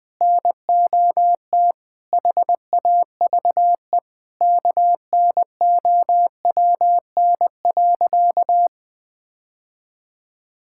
Morse code, heard as a keyed tone, about 20 wpm, 700 Hz. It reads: NOT HAVE KNOWN.
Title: Sentences with 3-Word Phrases - 20wpm